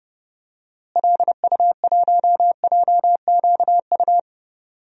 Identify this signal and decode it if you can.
Morse code, heard as a keyed tone, about 30 wpm, 700 Hz. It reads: LU1JQU
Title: DXCC-100 Call Signs - 6 Characters - Repeated 3x - 30wpm